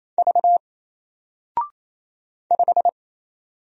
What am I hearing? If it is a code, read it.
Morse code, heard as a keyed tone, about 27 wpm, 700 Hz. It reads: V 5
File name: Single-Letter-Number-28wpm